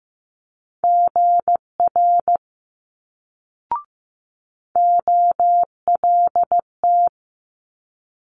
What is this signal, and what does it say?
Morse code, heard as a keyed tone, about 15 wpm, 700 Hz. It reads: GR OLT
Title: Multi-letter Phonemes Quick Fire 15wpm